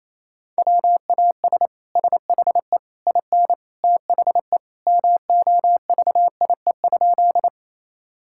Morse code, heard as a keyed tone, about 28 wpm, 700 Hz. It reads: WAS SHE IN THE MOVIE?
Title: Sentences from Top 1000 Words - Review Part 5 - 28wpm